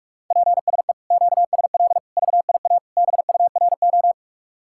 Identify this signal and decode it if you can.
Morse code, heard as a keyed tone, about 58 wpm, 700 Hz. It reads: PSE QSL VIA BURO